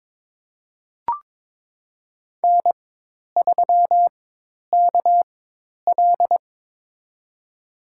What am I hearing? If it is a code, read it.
Morse code, heard as a keyed tone, about 21 wpm, spaced as for 10 wpm, 700 Hz. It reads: N3KL
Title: USA Call Signs - 4 Characters - 4x Character Spacing - 22wpm